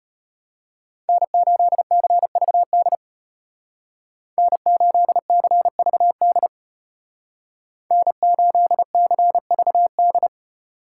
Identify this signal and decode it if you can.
Morse code, heard as a keyed tone, about 32 wpm, 700 Hz. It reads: N8CVD N8CVD N8CVD